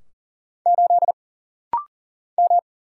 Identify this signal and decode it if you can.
Morse code, heard as a keyed tone, about 40 wpm, 700 Hz. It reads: 8 M
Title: Single Letter-Number - Rapid-Fire 40wpm